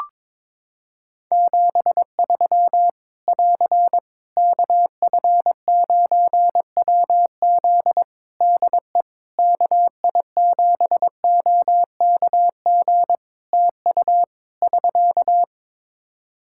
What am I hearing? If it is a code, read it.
Morse code, heard as a keyed tone, about 22 wpm, 700 Hz. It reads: 73 <AR> KF9WZ DE KI7OKG TU <SK>
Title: K1USN SST QSO Practice 22wpm